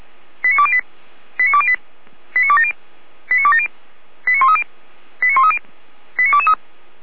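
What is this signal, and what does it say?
Five-tone selective call (ZVEI), heard as quick runs of five tones, tones of about 70 ms. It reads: 89208 89208 89270 89270 89130 89130 89223
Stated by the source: ZVEI five-tone selective call received by SDR, from the Feuerwehr Orlamünde audio archive.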